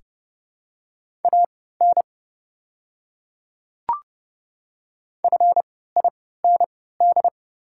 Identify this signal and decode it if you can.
Morse code, heard as a keyed tone, about 30 wpm, spaced as for 17 wpm, 690 Hz. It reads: AN FIND
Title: 3x Character Spacing - Top 300 Words - 30wpm